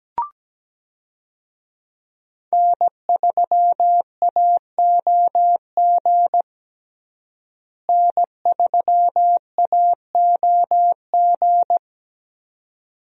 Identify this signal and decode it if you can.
Morse code, heard as a keyed tone, about 17 wpm, 700 Hz. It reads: N3AOG N3AOG
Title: USA Call Signs - 5 Characters - Repeated 2x - 17wpm